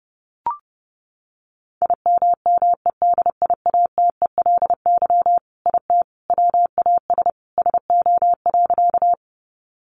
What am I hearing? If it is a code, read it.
Morse code, heard as a keyed tone, about 30 wpm, 700 Hz. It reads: IMMEDIATELY IT WAS SO.
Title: Sentences from Top 1100 Words 30wpm